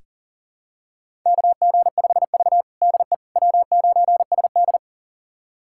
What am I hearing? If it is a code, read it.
Morse code, heard as a keyed tone, about 39 wpm, 700 Hz. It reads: KG5V DE W9SD
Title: USA Callsign DE Callsign - 4 Characters - 40wpm